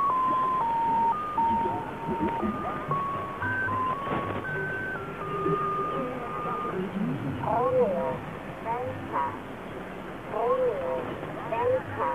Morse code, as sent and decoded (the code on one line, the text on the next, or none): none